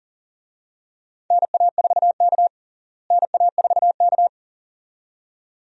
Morse code, heard as a keyed tone, about 40 wpm, 700 Hz. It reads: NA4K NA4K
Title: USA Call Signs - 4 Characters - Repeated 2x - 40wpm